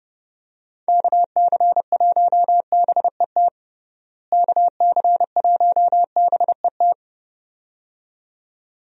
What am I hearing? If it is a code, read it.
Morse code, heard as a keyed tone, about 30 wpm, 700 Hz. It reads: KC1BET KC1BET
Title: USA Call Signs - 6 Characters - Repeated 2x - 30wpm